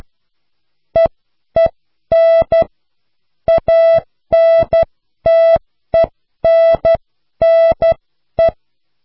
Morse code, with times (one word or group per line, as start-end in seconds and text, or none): 1.0-2.6 EEN
3.5-8.5 ANTENNE